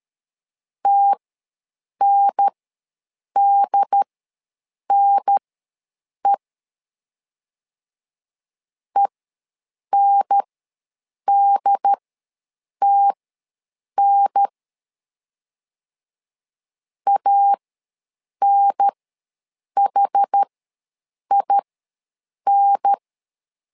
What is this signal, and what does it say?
Morse code, heard as a keyed tone, about 13 wpm, spaced as for 7 wpm, 800 Hz. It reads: TNDNE ENDTN ANHIN
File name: LEZIONI / 2 UHDN / 2 UHDN ws 12wpm 1708